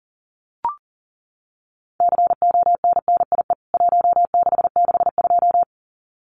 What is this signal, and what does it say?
Morse code, heard as a keyed tone, about 40 wpm, 700 Hz. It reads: CONNIE 1662
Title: CWT Exchange 40wpm